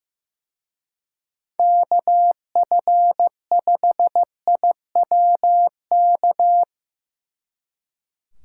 Morse code, heard as a keyed tone, about 16 wpm, 700 Hz. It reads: KF5IWK
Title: Speed Racing - USA 2x3 Call Signs 15wpm